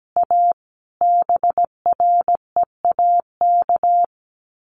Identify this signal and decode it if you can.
Morse code, heard as a keyed tone, about 17 wpm, 690 Hz. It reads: A BREAK
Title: Sets of 5 Words Encore 17wpm